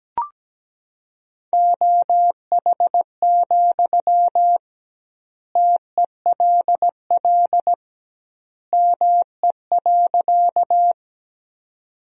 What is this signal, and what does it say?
Morse code, heard as a keyed tone, about 17 wpm, 700 Hz. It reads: OH, TELL ME.
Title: 2x Word-Spacing - Sentences from Top 500 Words - 17wpm